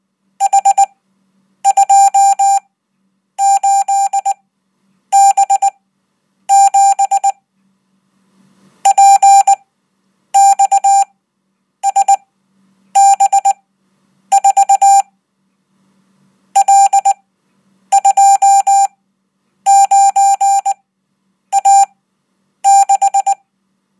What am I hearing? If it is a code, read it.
Morse code, heard as a keyed tone, about 19 wpm, spaced as for 8 wpm, 790 Hz. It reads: H28B7 PXSB4 L29A6
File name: Code